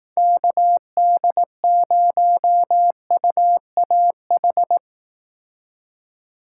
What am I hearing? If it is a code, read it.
Morse code, heard as a keyed tone, about 18 wpm, 700 Hz. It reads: KD0UAH
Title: Speed Racing - USA Call Signs 15wpm